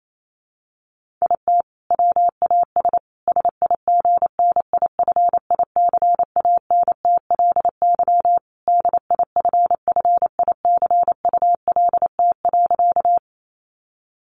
Morse code, heard as a keyed tone, about 28 wpm, 700 Hz. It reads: IT WAS SIGNIFICANTLY DIFFICULT.
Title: Sentences from Top 1500 Words 28wpm